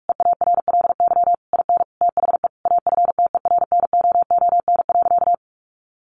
Morse code, heard as a keyed tone, about 44 wpm, 700 Hz. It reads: EARLY IN THE AFTERNOON.
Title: Sentences from Top 1500 Words 45wpm